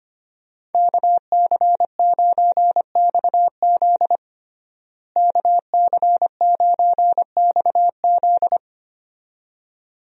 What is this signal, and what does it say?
Morse code, heard as a keyed tone, about 25 wpm, 700 Hz. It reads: KC9XZ KC9XZ
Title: USA Call Signs - 5 Characters - Repeated 2x - 25wpm